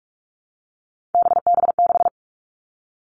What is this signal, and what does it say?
Morse code, heard as a keyed tone, about 46 wpm, 700 Hz. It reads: BB6
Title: Three Challenging Letter-Number Combinations 45wpm